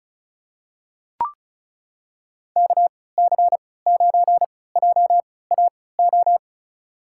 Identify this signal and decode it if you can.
Morse code, heard as a keyed tone, about 36 wpm, spaced as for 20 wpm, 700 Hz. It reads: KC9JAO
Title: USA Call Signs - 6 Characters - 3x Character Spacing - 35wpm